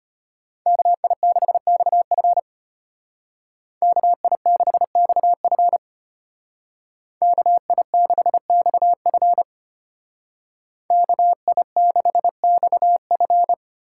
Morse code, heard as a keyed tone, about 32 wpm, 700 Hz. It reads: KI6XF KI6XF KI6XF KI6XF